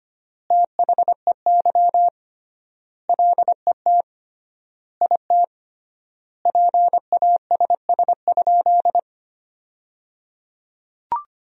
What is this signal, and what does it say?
Morse code, heard as a keyed tone, about 25 wpm, 700 Hz. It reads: THEY LET IT PASS?
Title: 3x Word-Spacing - Sentences from Top 400 Words - 25wpm